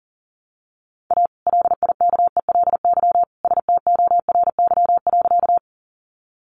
Morse code, heard as a keyed tone, about 40 wpm, 700 Hz. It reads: A LIKELY STORY.
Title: Sentences from Top 600 Words 40wpm